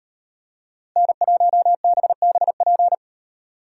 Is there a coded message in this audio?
Yes: Morse code from 1.0 to 2.9 s, heard as a keyed tone, about 38 wpm, 700 Hz. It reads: N1BBP